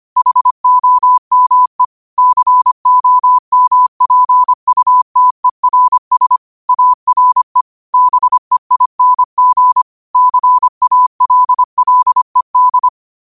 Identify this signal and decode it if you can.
Morse code, heard as a keyed tone, about 25 wpm, 1000 Hz. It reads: HOME COMPUTERS ARE BEING CALLED
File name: qotdfiles / 25wpm